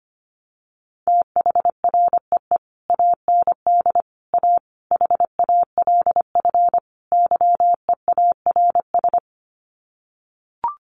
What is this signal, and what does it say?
Morse code, heard as a keyed tone, about 25 wpm, 700 Hz. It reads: THREE AND A HALF YEARS
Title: Sets of 5 Words Encore2 25wpm